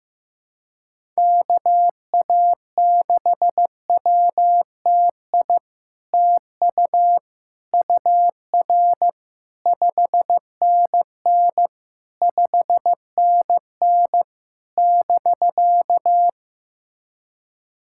Morse code, heard as a keyed tone, about 15 wpm, 700 Hz. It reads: KA6WTI TU UR 5NN 5NN <BK>
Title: SOTA QSO Practice 15wpm